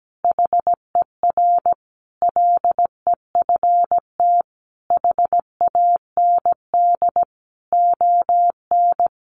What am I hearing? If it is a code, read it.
Morse code, heard as a keyed tone, about 17 wpm, 700 Hz. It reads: HER LEFT HAND ON
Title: Easy Phrases 17wpm